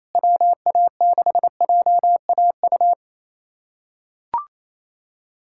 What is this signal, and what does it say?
Morse code, heard as a keyed tone, about 28 wpm, 700 Hz. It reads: WA6JAU
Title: USA Callsign DE Callsign - 6 Characters - 28wpm